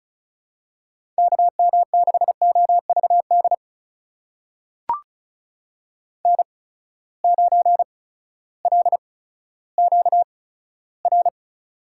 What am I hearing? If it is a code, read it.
Morse code, heard as a keyed tone, about 36 wpm, spaced as for 16 wpm, 700 Hz. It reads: KM6OVD N9LQR
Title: USA Call Signs - 8x Character Spacing - 35wpm